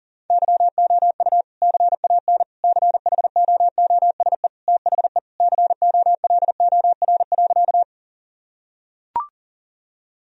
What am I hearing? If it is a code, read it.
Morse code, heard as a keyed tone, about 40 wpm, 700 Hz. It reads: YOU CAN CHOOSE THE COLOR.
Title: Sentences from Top 1000 Words - Review Part 4 - 40wpm